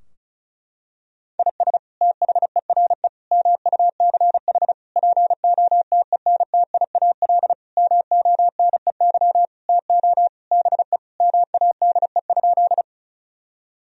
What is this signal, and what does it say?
Morse code, heard as a keyed tone, about 36 wpm, 700 Hz. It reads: IS THERE MUCH POTENTIAL MONEY TO BE MADE?
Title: Sentences from Top 1000 Words - Review Part 4 - 35wpm